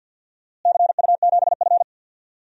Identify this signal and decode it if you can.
Morse code, heard as a keyed tone, about 48 wpm, 700 Hz. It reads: CU7F
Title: DXCC-100 Call Signs - 4 Characters - Repeated 2x - 50wpm